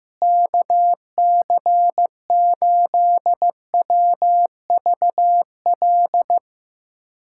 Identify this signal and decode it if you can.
Morse code, heard as a keyed tone, about 15 wpm, 700 Hz. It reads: KC8WVL
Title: USA Callsign DE Callsign 15wpm